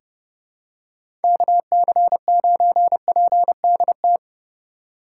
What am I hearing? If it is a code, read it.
Morse code, heard as a keyed tone, about 30 wpm, 700 Hz. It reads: KC9PDT
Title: USA Call Signs - 6 Characters - 8x Character Spacing - 30wpm